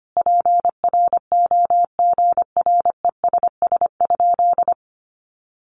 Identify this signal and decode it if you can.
Morse code, heard as a keyed tone, about 25 wpm, 700 Hz. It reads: PROGRESS?
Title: Sentences from Top 1400 Words 25wpm